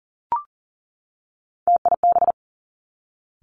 Morse code, heard as a keyed tone, about 39 wpm, 700 Hz. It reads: TIB
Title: Three Challenging Letter-Number Combinations 40wpm